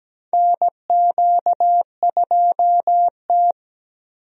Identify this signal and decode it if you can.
Morse code, heard as a keyed tone, about 17 wpm, 700 Hz. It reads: NQ2T
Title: USA Call Signs - 4 Characters - Repeated 3x - 17wpm